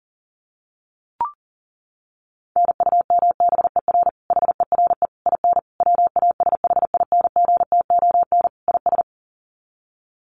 Morse code, heard as a keyed tone, about 40 wpm, 700 Hz. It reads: NUMBER HERE IN WASHINGTON IS
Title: Sets of 5 Words Encore2 40wpm